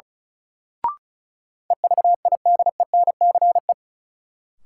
Morse code, heard as a keyed tone, about 34 wpm, 700 Hz. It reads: EVIDENCE
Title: Sets of 5 Words Encore - New Word Drill - 35wpm